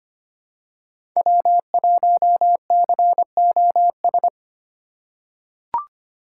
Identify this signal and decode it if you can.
Morse code, heard as a keyed tone, about 25 wpm, 700 Hz. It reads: W1COS
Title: USA Call Signs - 5 Characters - 2x Character Spacing - 25wpm